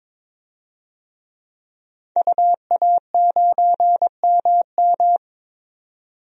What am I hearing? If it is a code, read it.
Morse code, heard as a keyed tone, about 22 wpm, 700 Hz. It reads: UA9MM
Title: DXCC-100 Call Signs - 5 Characters - 22wpm